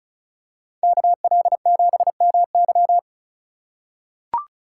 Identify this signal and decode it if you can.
Morse code, heard as a keyed tone, about 36 wpm, 700 Hz. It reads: KL7MY
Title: USA Call Signs - 5 Characters - 2x Character Spacing - 35wpm